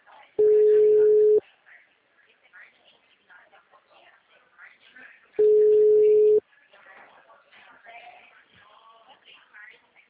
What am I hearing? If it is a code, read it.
Ringback tone (Europe, 425 Hz), with speech.